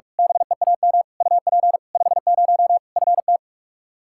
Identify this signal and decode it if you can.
Morse code, heard as a keyed tone, about 46 wpm, 700 Hz. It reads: BEAM UP 50 FT
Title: Practice QSO - Exchange 2 - Info - 45wpm